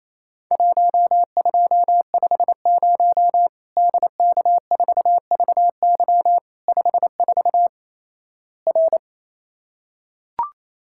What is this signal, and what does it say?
Morse code, heard as a keyed tone, about 28 wpm, 700 Hz. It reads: 1250 DK4VY 54 R
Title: Worked All Europe DX Contest - QTC Exchange - 28wpm